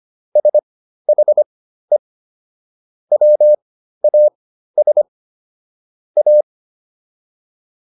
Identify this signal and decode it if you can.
Morse code, heard as a keyed tone, about 25 wpm, spaced as for 13 wpm, 600 Hz. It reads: SHE WAS A